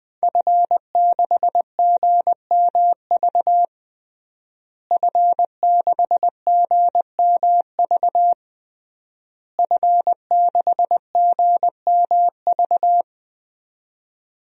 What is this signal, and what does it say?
Morse code, heard as a keyed tone, about 21 wpm, 700 Hz. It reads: F6GMV F6GMV F6GMV
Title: DXCC-100 Call Signs - 5 Characters - Repeated 3x - 20wpm